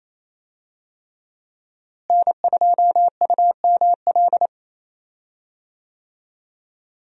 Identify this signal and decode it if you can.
Morse code, heard as a keyed tone, about 28 wpm, 700 Hz. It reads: N2UML
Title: USA Call Signs - 5 Characters - Repeated 3x - 28wpm